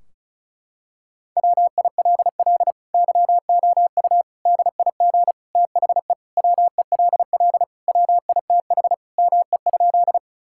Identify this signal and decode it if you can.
Morse code, heard as a keyed tone, about 34 wpm, 700 Hz. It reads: WILL YOU DIG THE WELL WITH ME?
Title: Sentences from Top 1600 Words 35wpm